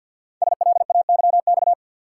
Morse code, heard as a keyed tone, about 48 wpm, 700 Hz. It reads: SLAYX
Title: Code Groups - 5 Letters - 50wpm